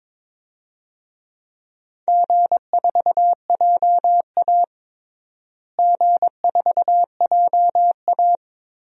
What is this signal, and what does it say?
Morse code, heard as a keyed tone, about 22 wpm, 700 Hz. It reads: G4JA G4JA
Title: DXCC-100 Call Signs - 4 Characters - Repeated 2x - 22wpm